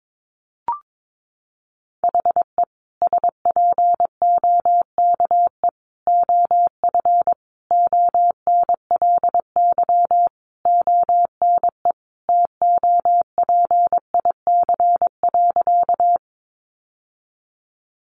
Morse code, heard as a keyed tone, about 22 wpm, 700 Hz. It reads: HE SPOKE OF ONLY ONE TOPIC.
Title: Sentences from Top 1200 Words 22wpm